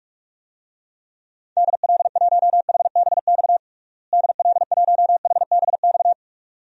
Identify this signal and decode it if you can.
Morse code, heard as a keyed tone, about 43 wpm, 700 Hz. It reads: DL1HBX DL1HBX
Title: DXCC-100 Call Signs - 6 Characters - Repeated 2x - 45wpm